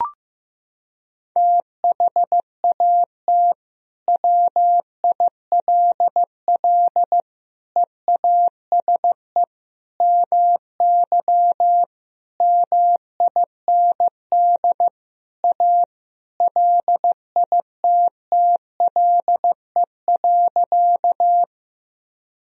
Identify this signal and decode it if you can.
Morse code, heard as a keyed tone, about 15 wpm, 700 Hz. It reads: THAT WILL EASE MY MIND A LITTLE.
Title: Sentences from Top 1000 Words - Review Part 1 - 15wpm